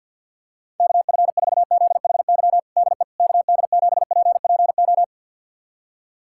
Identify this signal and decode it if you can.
Morse code, heard as a keyed tone, about 52 wpm, 690 Hz. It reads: KF4ZHY DE KD7PPO
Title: USA Callsign DE Callsign 50wpm